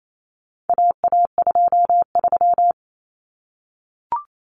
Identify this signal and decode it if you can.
Morse code, heard as a keyed tone, about 28 wpm, 700 Hz. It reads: AA23
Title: Common Cut Numbers - 4-Digit 28wpm